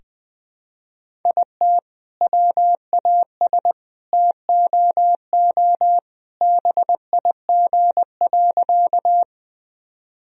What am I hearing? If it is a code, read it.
Morse code, heard as a keyed tone, about 21 wpm, 700 Hz. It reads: IT WAS TOO BIG.
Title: Sentences from Top 1000 Words - Review Part 1 - 20wpm